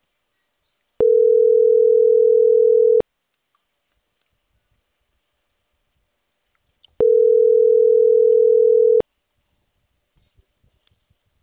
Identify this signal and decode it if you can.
Ringback tone (North America), 440 + 480 Hz.